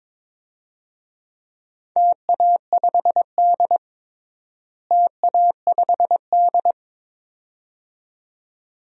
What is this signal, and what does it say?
Morse code, heard as a keyed tone, about 22 wpm, 700 Hz. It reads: TA5D TA5D
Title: DXCC-100 Call Signs - 4 Characters - Repeated 2x - 22wpm